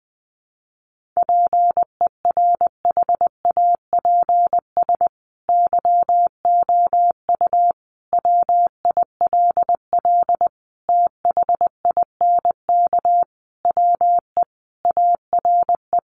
Morse code, heard as a keyed tone, about 20 wpm, 700 Hz. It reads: PERHAPS YOU WILL THINK WE ARE
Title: Sentences from Top 1400 Words 20wpm